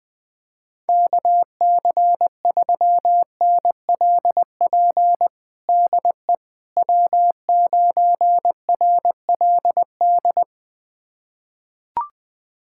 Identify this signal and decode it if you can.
Morse code, heard as a keyed tone, about 21 wpm, 700 Hz. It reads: KC3NLP DE W9RLD
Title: USA Callsign DE Callsign 20wpm